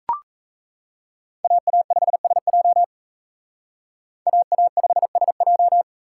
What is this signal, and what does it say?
Morse code, heard as a keyed tone, about 35 wpm, 700 Hz. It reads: AA5SJ AA5SJ